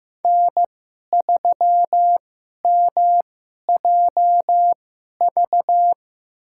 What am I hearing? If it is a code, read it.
Morse code, heard as a keyed tone, about 15 wpm, spaced as for 11 wpm, 700 Hz. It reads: N3MJV
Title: USA Call Signs - 2x Character Spacing - 15wpm